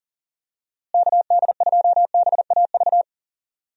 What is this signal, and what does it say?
Morse code, heard as a keyed tone, about 40 wpm, 700 Hz. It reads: KD2BAV
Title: USA Call Signs - 6 Characters - Repeated 3x - 40wpm